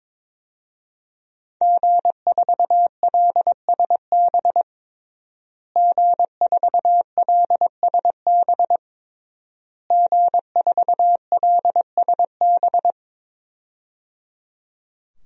Morse code, heard as a keyed tone, about 22 wpm, 700 Hz. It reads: G4LSB G4LSB G4LSB
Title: DXCC-100 Call Signs - 5 Characters - Repeated 3x - 22wpm